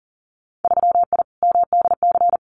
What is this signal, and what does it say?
Morse code, heard as a keyed tone, about 41 wpm, 700 Hz. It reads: 3I MDC
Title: Winter Field Day Exchanges Encore 40wpm